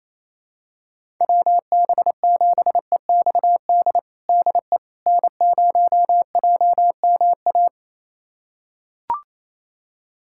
Morse code, heard as a keyed tone, about 28 wpm, 700 Hz. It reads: WB7EXD DE N0JMA
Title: USA Callsign DE Callsign 28wpm